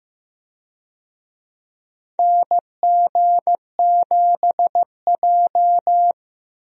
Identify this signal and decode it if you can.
Morse code, heard as a keyed tone, about 15 wpm, 700 Hz. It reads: NG7J
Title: USA Call Signs - 4 Characters - 15wpm